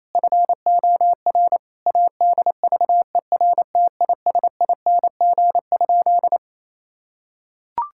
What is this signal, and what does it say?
Morse code, heard as a keyed tone, about 28 wpm, 700 Hz. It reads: FOR ADVERTISING?
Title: Sentences from Top 1000 Words - Review Part 6 - 28wpm